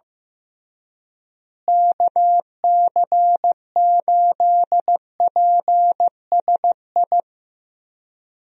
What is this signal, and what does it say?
Morse code, heard as a keyed tone, about 15 wpm, 700 Hz. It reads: KC8PSI
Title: USA Call Signs - 6 Characters - Repeated 3x - 15wpm